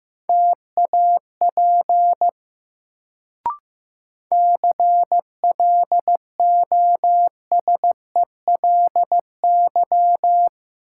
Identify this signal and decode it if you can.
Morse code, heard as a keyed tone, about 15 wpm, 700 Hz. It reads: TAP CLOSELY
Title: Sentences from Top 1400 Words - New Word Drill - 15wpm